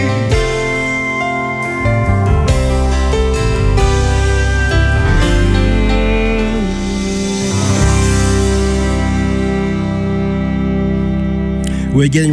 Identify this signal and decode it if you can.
Dial-up modem handshake, with speech.